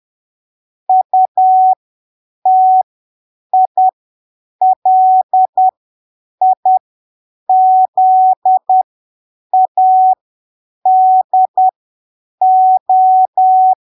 Morse code, heard as a keyed tone, about 10 wpm, spaced as for 7 wpm, 750 Hz. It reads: UTILIZADO
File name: arquivos / texto / texto 14 10ppm